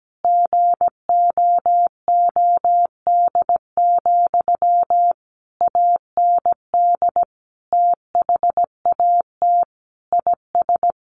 Morse code, heard as a keyed tone, about 17 wpm, 700 Hz. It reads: GOOD, AND THAT IS
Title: Sentences from Top 1400 Words 17wpm